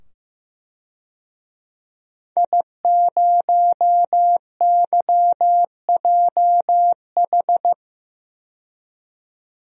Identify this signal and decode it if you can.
Morse code, heard as a keyed tone, about 15 wpm, 700 Hz. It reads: I0YJH
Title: DXCC-100 Call Signs - 5 Characters - 15wpm